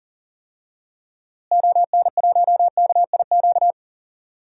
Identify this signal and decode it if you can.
Morse code, heard as a keyed tone, about 40 wpm, 700 Hz. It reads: ON1KIQ
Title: DXCC-100 Call Signs - 6 Characters - Repeated 2x - 40wpm